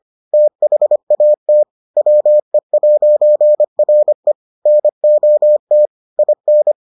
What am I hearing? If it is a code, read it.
Morse code, heard as a keyed tone, about 25 wpm, 600 Hz. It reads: THAT WE'RE NOT IN